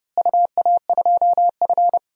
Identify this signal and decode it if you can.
Morse code, heard as a keyed tone, about 30 wpm, 700 Hz. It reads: UA2F
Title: DXCC-100 Call Signs - 4 Characters - Repeated 3x - 30wpm